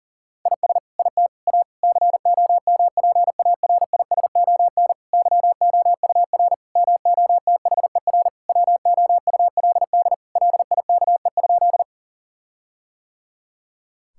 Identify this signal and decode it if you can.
Morse code, heard as a keyed tone, about 37 wpm, 700 Hz. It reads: IS IT A COMPARISON YOUR MOTHER WOULD LIKE?
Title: Sentences from Top 1600 Words 40wpm